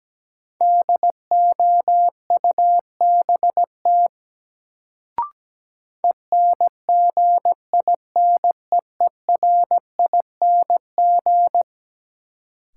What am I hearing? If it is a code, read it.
Morse code, heard as a keyed tone, about 17 wpm, 700 Hz. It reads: DOUBT ENGINEERING
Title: Sentences from Top 1500 Words - New Word Drill - 17wpm